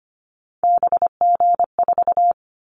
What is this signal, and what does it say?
Morse code, heard as a keyed tone, about 25 wpm, 700 Hz. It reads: BG4
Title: Callsign Prefixes for Top 100 DXCC - 25wpm